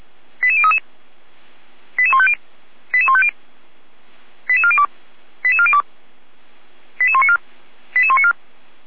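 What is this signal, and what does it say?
Five-tone selective call (ZVEI), heard as quick runs of five tones, tones of about 80 ms. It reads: 88033 88150 88150 88402 88402 88194 88194
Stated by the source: ZVEI five-tone selective call 88102, received on an SDR, from the Feuerwehr Orlamünde server.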